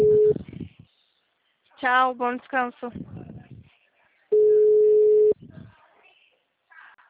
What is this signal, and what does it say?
Ringback tone (Europe, 425 Hz), with speech.